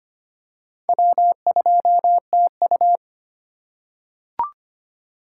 Morse code, heard as a keyed tone, about 25 wpm, 700 Hz. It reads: W2TU
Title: USA Call Signs - 4 Characters - 8x Character Spacing - 25wpm